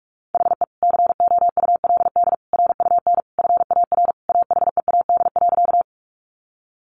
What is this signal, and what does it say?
Morse code, heard as a keyed tone, about 44 wpm, 700 Hz. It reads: HE COULD RUN FAR AHEAD.
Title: Sentences from Top 1100 Words 45wpm